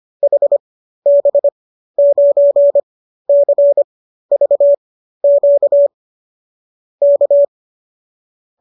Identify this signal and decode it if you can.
Morse code, heard as a keyed tone, about 25 wpm, spaced as for 13 wpm, 570 Hz. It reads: HB9CVQ K